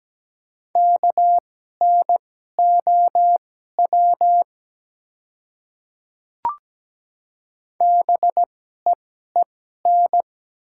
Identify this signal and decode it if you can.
Morse code, heard as a keyed tone, about 17 wpm, spaced as for 12 wpm, 700 Hz. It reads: KNOW BEEN
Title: 2x Character Spacing - Top 200 Words - 17wpm